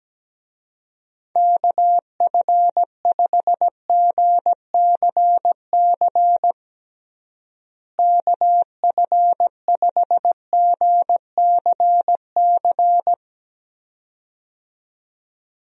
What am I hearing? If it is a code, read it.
Morse code, heard as a keyed tone, about 17 wpm, 700 Hz. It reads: KF5GCC KF5GCC
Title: USA Call Signs - 6 Characters - Repeated 2x - 17wpm